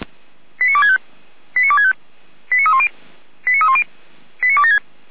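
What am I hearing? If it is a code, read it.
Five-tone selective call (ZVEI), heard as quick runs of five tones, tones of about 70 ms. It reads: 89265 89265 89310 89310 89276
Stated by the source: ZVEI five-tone call received by SDR, from the feuerwehr-orlamuende.org server (fire brigade Orlamünde).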